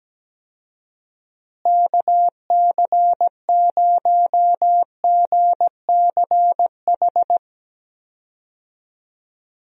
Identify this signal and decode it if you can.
Morse code, heard as a keyed tone, about 17 wpm, 700 Hz. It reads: KC0GCH
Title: USA Call Signs - 6 Characters - Repeated 3x - 17wpm